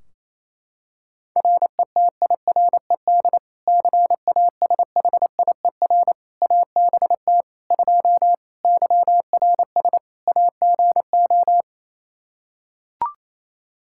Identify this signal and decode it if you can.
Morse code, heard as a keyed tone, about 27 wpm, 700 Hz. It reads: RETIRED CASHIER ABT 2 YRS AGO
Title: K1USN SST QSO Practice 28wpm